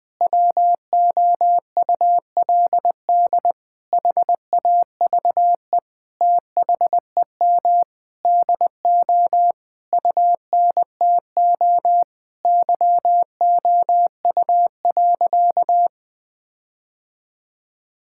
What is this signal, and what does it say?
Morse code, heard as a keyed tone, about 20 wpm, 700 Hz. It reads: WOULD HAVE THEM DO UNTO YOU.
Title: Sentences with 9-Word Phrases - 20wpm